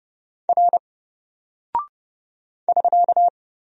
Morse code, heard as a keyed tone, about 29 wpm, 700 Hz. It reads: R <SK>
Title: Single Character (Letter:Number:Punctuation:Prosign) 30wpm